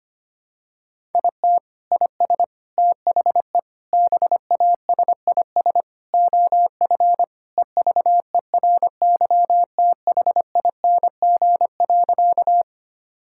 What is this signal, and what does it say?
Morse code, heard as a keyed tone, about 25 wpm, 700 Hz. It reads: IT IS THE BASIS OF EVERYTHING.
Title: Sentences from Top 1000 Words - Review Part 5 - 25wpm